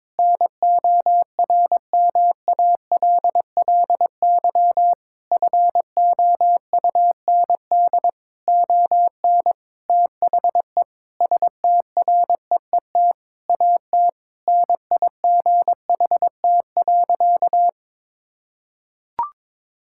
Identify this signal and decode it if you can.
Morse code, heard as a keyed tone, about 22 wpm, 700 Hz. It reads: NORMALLY FOUND ON THE STREET AT NIGHT.
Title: Sentences from Top 1000 Words - Review Part 6 - 22wpm